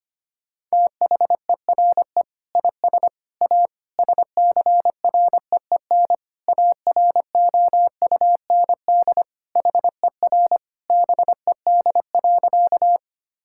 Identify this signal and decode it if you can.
Morse code, heard as a keyed tone, about 25 wpm, 700 Hz. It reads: THERE IS A SCREEN AROUND HER BED.
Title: Sentences from Top 1000 Words - Review Part 6 - 25wpm